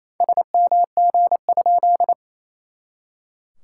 Morse code, heard as a keyed tone, about 27 wpm, 700 Hz. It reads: SMG?
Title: SOTA QSO Practice 28wpm